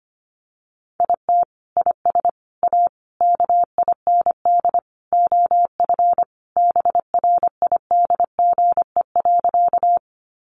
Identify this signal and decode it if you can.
Morse code, heard as a keyed tone, about 25 wpm, 700 Hz. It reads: IT IS A KIND OF BRIDGE.
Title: Sentences from Top 1200 Words 25wpm